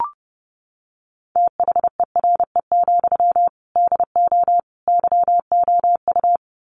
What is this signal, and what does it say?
Morse code, heard as a keyed tone, about 30 wpm, 700 Hz. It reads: THERE, DO YOU
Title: Sentences from Top 300 Words 30wpm